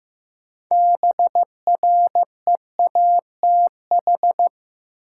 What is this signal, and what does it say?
Morse code, heard as a keyed tone, about 15 wpm, 700 Hz. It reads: BREATH
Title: Sets of 5 Words Encore - New Word Drill - 15wpm